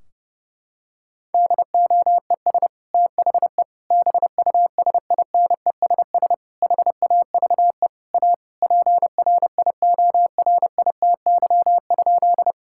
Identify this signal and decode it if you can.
Morse code, heard as a keyed tone, about 30 wpm, 700 Hz. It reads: DOES THE BUSINESS HAVE A PRIORITY?
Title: Sentences from Top 1600 Words 30wpm